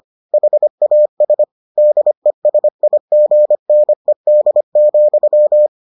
Morse code, heard as a keyed tone, about 25 wpm, 600 Hz. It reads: HAS DESIGNED,